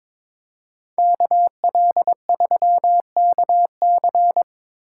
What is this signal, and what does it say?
Morse code, heard as a keyed tone, about 26 wpm, 700 Hz. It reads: KL3KC
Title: Speed Racing - USA 2x2 Call Signs 22wpm